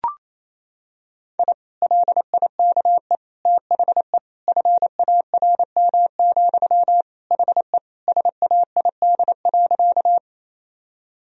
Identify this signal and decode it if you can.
Morse code, heard as a keyed tone, about 28 wpm, 700 Hz. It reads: I LIKE THE FARM, HE SAID.
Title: Sentences from Top 1000 Words - Review Part 2 - 28wpm